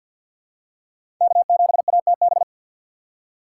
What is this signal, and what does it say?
Morse code, heard as a keyed tone, about 49 wpm, 700 Hz. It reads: K7ATB
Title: USA Call Signs - 5 Characters - 6x Character Spacing - 50wpm